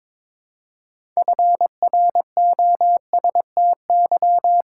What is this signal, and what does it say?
Morse code, heard as a keyed tone, about 22 wpm, 700 Hz. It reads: FROSTY
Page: morse-code-ninja-practice-22wpm